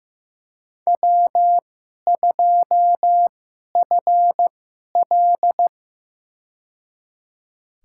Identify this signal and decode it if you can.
Morse code, heard as a keyed tone, about 15 wpm, spaced as for 11 wpm, 700 Hz. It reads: W2FL
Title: USA Call Signs - 4 Characters - 2x Character Spacing - 15wpm